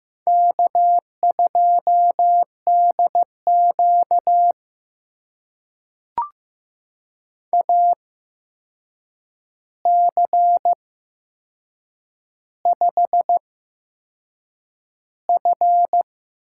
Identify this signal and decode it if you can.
Morse code, heard as a keyed tone, about 15 wpm, spaced as for 6 wpm, 700 Hz. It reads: K2DQ AC5F
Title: USA Call Signs - 4 Characters - 8x Character Spacing - 15wpm